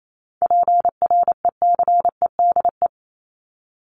Morse code, heard as a keyed tone, about 28 wpm, 700 Hz. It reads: PRECEDE
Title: ING Suffix - Top 100 Words with Base Words 28wpm